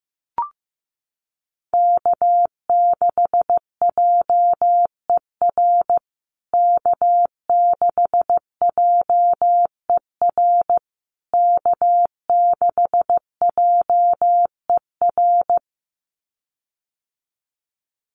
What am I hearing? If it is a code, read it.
Morse code, heard as a keyed tone, about 15 wpm, 700 Hz. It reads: K6JER K6JER K6JER
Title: USA Call Signs Repeated 3x 15wpm